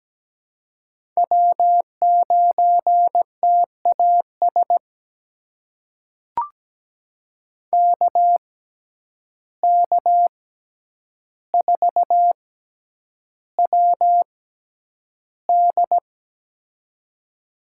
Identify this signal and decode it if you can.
Morse code, heard as a keyed tone, about 17 wpm, spaced as for 9 wpm, 700 Hz. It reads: W9TAS KK4WD
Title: USA Call Signs - 5 Characters - 6x Character Spacing - 17wpm